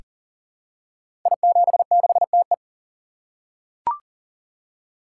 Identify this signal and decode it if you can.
Morse code, heard as a keyed tone, about 38 wpm, 700 Hz. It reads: I76TE
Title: Code Groups - 5 Letter:Numbers - 40wpm